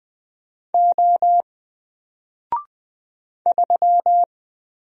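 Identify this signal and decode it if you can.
Morse code, heard as a keyed tone, about 20 wpm, 700 Hz. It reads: O 3
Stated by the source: Single letter and number Morse drill at 20 wpm, from N4LCD.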